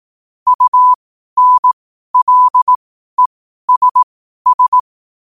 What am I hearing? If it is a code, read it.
Morse code, heard as a keyed tone, about 18 wpm, spaced as for 12 wpm, 1000 Hz. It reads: UNLESS
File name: qotdfiles / 12wpm